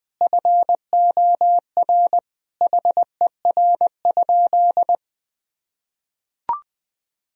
Morse code, heard as a keyed tone, about 20 wpm, 700 Hz. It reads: FOR HER?
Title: Sentences from Top 1000 Words - Review Part 1 - 20wpm